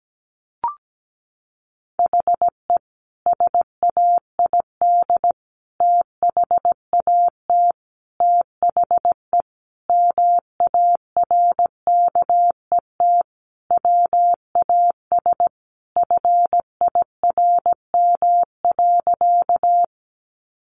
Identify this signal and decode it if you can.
Morse code, heard as a keyed tone, about 17 wpm, 700 Hz. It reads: HE SAID THAT THE MARKET WAS FIRM.
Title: Sentences from Top 1400 Words 17wpm